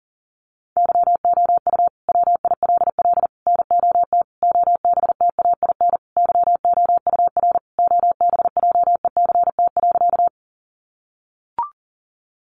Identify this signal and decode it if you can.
Morse code, heard as a keyed tone, about 40 wpm, 700 Hz. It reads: YOU WILL NOT OBTAIN YOUR OBJECT.
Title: Sentences from Top 1300 Words 40wpm